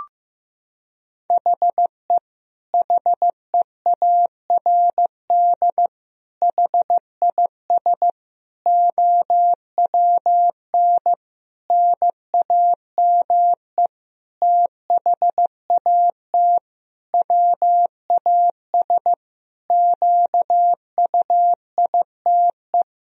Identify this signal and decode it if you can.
Morse code, heard as a keyed tone, about 15 wpm, 700 Hz. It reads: HE HEARD HIS OWN NAME THAT WAS QUITE
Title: Sentences from Top 1000 Words - Review Part 2 - 15wpm